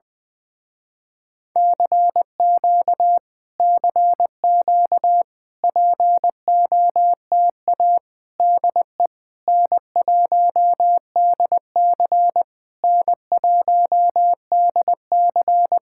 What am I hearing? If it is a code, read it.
Morse code, heard as a keyed tone, about 21 wpm, 700 Hz. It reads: CQ CQ POTA DE N1DC N1DC
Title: Speed-Racing Common QSO Phrases - 20wpm